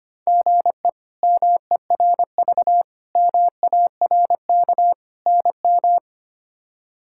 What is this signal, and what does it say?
Morse code, heard as a keyed tone, about 25 wpm, 700 Hz. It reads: GE MERV MARK NM
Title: K1USN SST QSO Practice - 25wpm